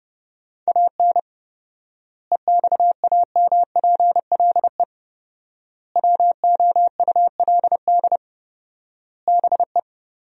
Morse code, heard as a keyed tone, about 30 wpm, 700 Hz. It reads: AN EXAMPLE WOULD BE
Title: 4x Word-Spacing - Sentences from Top 400 Words - 30wpm